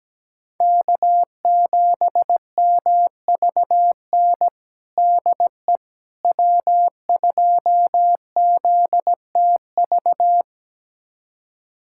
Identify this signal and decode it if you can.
Morse code, heard as a keyed tone, about 17 wpm, 700 Hz. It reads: K7MVN DE W2ZTV
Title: USA Callsign DE Callsign 17wpm